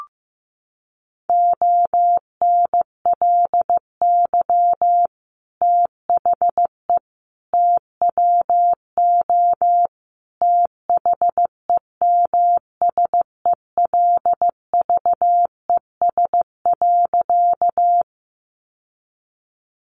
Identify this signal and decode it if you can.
Morse code, heard as a keyed tone, about 15 wpm, 700 Hz. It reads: ONLY THE TWO THEMSELVES.
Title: Sentences from Top 800 Words 15wpm